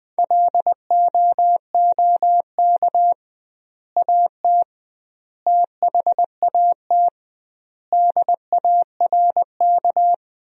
Morse code, heard as a keyed tone, about 20 wpm, 700 Hz. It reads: LOOK AT THAT DARK
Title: 2x Word-Spacing - Sentences from Top 500 Words - 20wpm